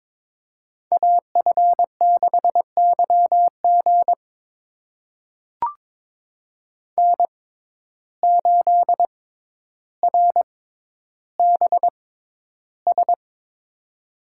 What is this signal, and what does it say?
Morse code, heard as a keyed tone, about 23 wpm, spaced as for 11 wpm, 700 Hz. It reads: AF6YG N8RBS
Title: USA Call Signs - 5 Characters - 6x Character Spacing - 22wpm